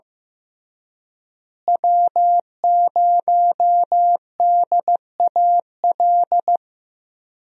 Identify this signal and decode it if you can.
Morse code, heard as a keyed tone, about 15 wpm, 700 Hz. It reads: W0DAL
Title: USA Call Signs - 5 Characters - Repeated 3x - 15wpm